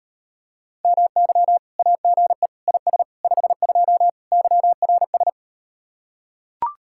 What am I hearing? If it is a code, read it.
Morse code, heard as a keyed tone, about 38 wpm, 700 Hz. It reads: MY AGE IS 52 YRS